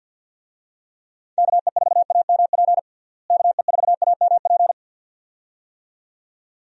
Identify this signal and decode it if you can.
Morse code, heard as a keyed tone, about 52 wpm, 700 Hz. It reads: KE4AMP KE4AMP
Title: USA Call Signs - 6 Characters - Repeated 2x - 50wpm